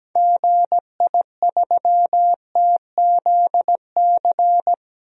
Morse code, heard as a keyed tone, about 17 wpm, 700 Hz. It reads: GI3TZC
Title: DXCC-100 Call Signs - 6 Characters - 17wpm